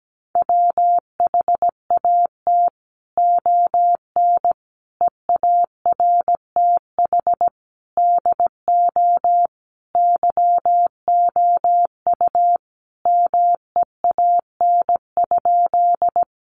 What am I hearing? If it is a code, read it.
Morse code, heard as a keyed tone, about 17 wpm, 700 Hz. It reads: WHAT ON EARTH DO YOU MEAN?
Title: Sentences from Top 200 Words 17wpm